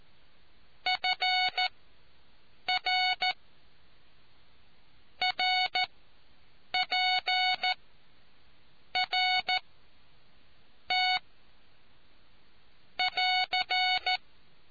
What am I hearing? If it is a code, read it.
Morse code, heard as a keyed tone, about 13 wpm, spaced as for 6 wpm, 740 Hz. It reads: FR RPRT <AR>